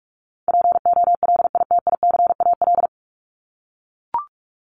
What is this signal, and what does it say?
Morse code, heard as a keyed tone, about 44 wpm, 700 Hz. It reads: POLITICAL
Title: Top 501-750 Words 45wpm